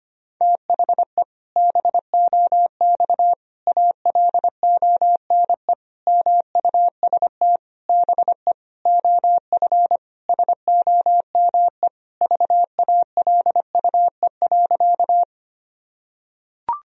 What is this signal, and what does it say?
Morse code, heard as a keyed tone, about 25 wpm, 700 Hz. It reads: THE BOX ALONE MUST BE OF SOME VALUE.
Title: Sentences from Top 1000 Words - Review Part 3 - 25wpm